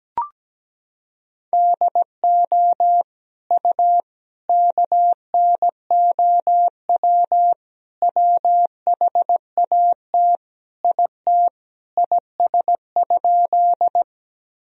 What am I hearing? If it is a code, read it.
Morse code, heard as a keyed tone, about 17 wpm, 700 Hz. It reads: DO U KNOW WHAT IT IS?
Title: Sentences from Top 100 Words and Abbreviations 17wpm